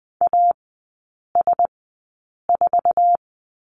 Morse code, heard as a keyed tone, about 20 wpm, spaced as for 8 wpm, 700 Hz. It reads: AS4
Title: 047 Lesson 9e - Number 4 - Three Characters 20wpm